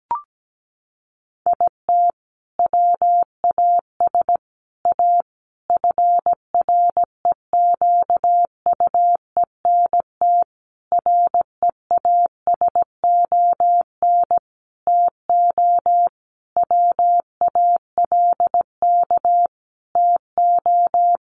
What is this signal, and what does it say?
Morse code, heard as a keyed tone, about 17 wpm, 700 Hz. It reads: IT WAS A FREQUENT REASON TO WALK TO
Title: Sentences from Top 1400 Words 17wpm